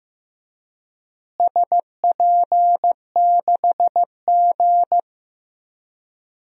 Morse code, heard as a keyed tone, about 15 wpm, 700 Hz. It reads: SP6G
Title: DXCC-100 Call Signs - 4 Characters - Repeated 3x - 15wpm